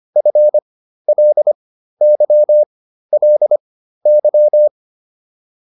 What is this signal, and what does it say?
Morse code, heard as a keyed tone, about 25 wpm, spaced as for 13 wpm, 600 Hz. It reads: FLYLY
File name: LY5 13